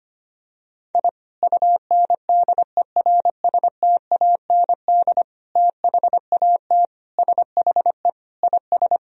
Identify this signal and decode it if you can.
Morse code, heard as a keyed tone, about 25 wpm, 700 Hz. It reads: I UNDERSTAND THAT SHE IS
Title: Sentences from Top 1000 Words - Review Part 3 - 25wpm